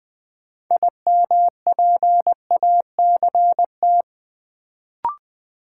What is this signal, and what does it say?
Morse code, heard as a keyed tone, about 20 wpm, 700 Hz. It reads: IMPACT
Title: Sentences from Top 1000 Words - Review Part 6 - New Word Drill - 20wpm